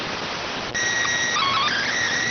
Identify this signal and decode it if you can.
SSTV (slow-scan television), PD 290.